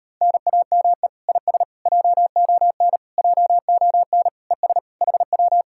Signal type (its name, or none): Morse code